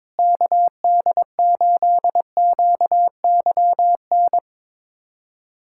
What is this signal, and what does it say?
Morse code, heard as a keyed tone, about 22 wpm, 700 Hz. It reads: KD8QYN
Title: USA Callsign DE Callsign - 6 Characters - 22wpm